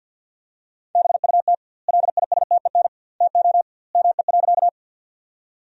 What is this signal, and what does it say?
Morse code, heard as a keyed tone, about 50 wpm, 700 Hz. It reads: BUT LISTEN TO ME.
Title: 2x Word-Spacing - Sentences from Top 400 Words - 50wpm